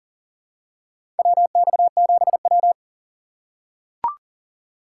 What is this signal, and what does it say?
Morse code, heard as a keyed tone, about 39 wpm, 700 Hz. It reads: WX7W
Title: USA Call Signs - 6x Character Spacing - 40wpm